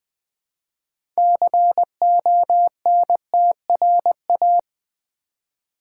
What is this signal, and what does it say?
Morse code, heard as a keyed tone, about 20 wpm, 700 Hz. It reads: CONTRA
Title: Build A Word - 2 Letters At A Time - 8-Letter Words - 20wpm